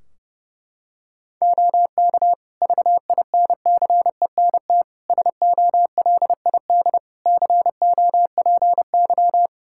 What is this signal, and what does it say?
Morse code, heard as a keyed tone, about 30 wpm, 700 Hz. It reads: OK VINCENT SOLID COPY
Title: Practice QSO - Exchange 3 - More Info - 30wpm